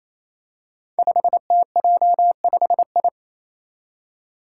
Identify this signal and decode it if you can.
Morse code, heard as a keyed tone, about 28 wpm, 700 Hz. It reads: 5TJ5I
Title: Five Challenging Letter-Number Combinations 28wpm